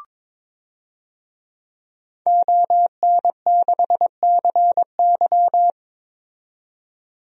Morse code, heard as a keyed tone, about 22 wpm, 700 Hz. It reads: ON6CY
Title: DXCC-100 Call Signs - 5 Characters - 22wpm